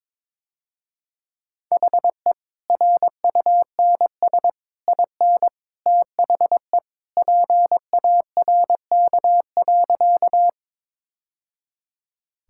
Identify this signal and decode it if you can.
Morse code, heard as a keyed tone, about 22 wpm, 700 Hz. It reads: HE RUNS IN THE PARK.
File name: practice / 22wpm / Easy Sentences with Hints - 22wpm